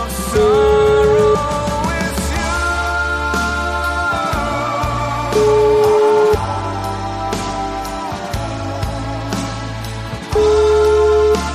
Ringback tone (Europe, 425 Hz), with noise.